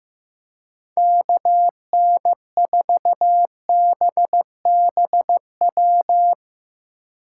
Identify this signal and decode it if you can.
Morse code, heard as a keyed tone, about 17 wpm, 700 Hz. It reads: KN4BBW
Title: Speed Racing - USA 2x3 Call Signs 15wpm